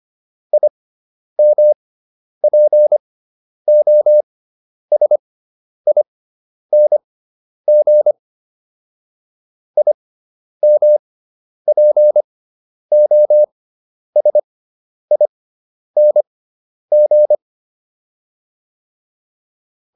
Morse code, heard as a keyed tone, about 25 wpm, spaced as for 10 wpm, 600 Hz. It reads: IMPOSING IMPOSING